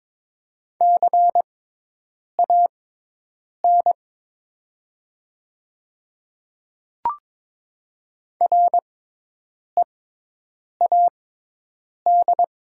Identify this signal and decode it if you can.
Morse code, heard as a keyed tone, about 22 wpm, spaced as for 8 wpm, 700 Hz. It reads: CAN READ
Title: 6x Character Spacing - Top 300 Words - 22wpm